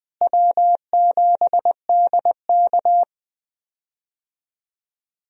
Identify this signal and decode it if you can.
Morse code, heard as a keyed tone, about 21 wpm, 700 Hz. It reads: W7DK
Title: USA Call Signs - 4 Characters - Repeated 2x - 20wpm